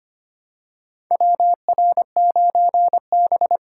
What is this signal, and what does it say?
Morse code, heard as a keyed tone, about 25 wpm, 700 Hz. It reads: WR9B
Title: USA Call Signs - 4 Characters - 2x Character Spacing - 25wpm